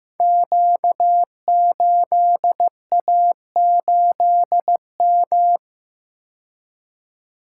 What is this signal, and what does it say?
Morse code, heard as a keyed tone, about 15 wpm, 700 Hz. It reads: Q8A8M
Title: Code Groups - 5 Letter:Numbers - 15wpm